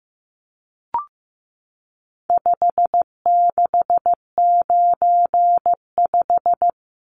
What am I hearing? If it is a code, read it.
Morse code, heard as a keyed tone, about 15 wpm, 700 Hz. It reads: 5695
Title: Numbers - 4-Digit 15wpm